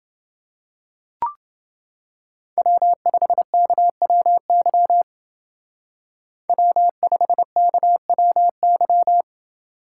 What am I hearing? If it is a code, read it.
Morse code, heard as a keyed tone, about 28 wpm, 700 Hz. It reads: W5KWY W5KWY